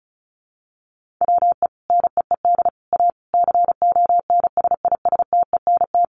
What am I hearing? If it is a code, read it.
Morse code, heard as a keyed tone, about 34 wpm, 700 Hz. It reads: WE NEED A CONSISTENT
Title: Sentences from Top 1500 Words 35wpm